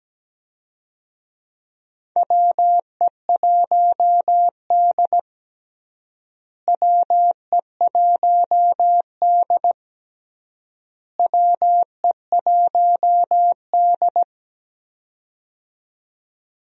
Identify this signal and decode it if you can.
Morse code, heard as a keyed tone, about 17 wpm, 700 Hz. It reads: WE1D WE1D WE1D
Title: USA Call Signs - 4 Characters - Repeated 3x - 17wpm